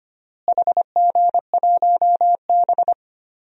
Morse code, heard as a keyed tone, about 25 wpm, 700 Hz. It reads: HG1B
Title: DXCC-100 Call Signs - 4 Characters - Repeated 3x - 25wpm